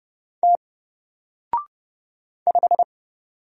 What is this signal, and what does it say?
Morse code, heard as a keyed tone, about 31 wpm, 700 Hz. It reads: T 5
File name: Single-Letter-Number-30wpm